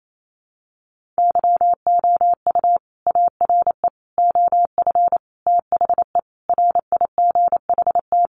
Morse code, heard as a keyed tone, about 28 wpm, 700 Hz. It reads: YOU ARE OF THE RIGHT
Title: Sentences from Top 1400 Words 28wpm